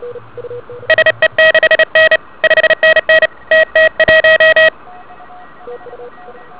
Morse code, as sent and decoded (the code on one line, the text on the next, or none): ... -.... -. / ..... -. -. / - - .----
SE6N 5NN TT1